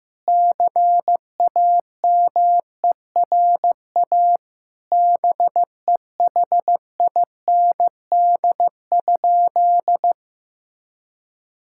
Morse code, heard as a keyed tone, about 16 wpm, 700 Hz. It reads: CAMERA BEHIND?
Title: Sentences from Top 1000 Words - Review Part 5 - 15wpm